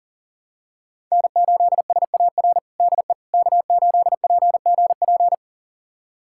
Morse code, heard as a keyed tone, about 40 wpm, 700 Hz. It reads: N8SAR DE K8PGP
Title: USA Callsign DE Callsign - 5 Characters - 40wpm